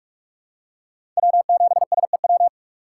Morse code, heard as a keyed tone, about 46 wpm, 700 Hz. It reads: W7SEW
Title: USA Call Signs - 5 Characters - 3x Character Spacing - 45wpm